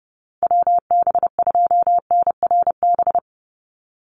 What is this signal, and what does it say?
Morse code, heard as a keyed tone, about 30 wpm, 700 Hz. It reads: WB2NRB
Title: USA Call Signs 30wpm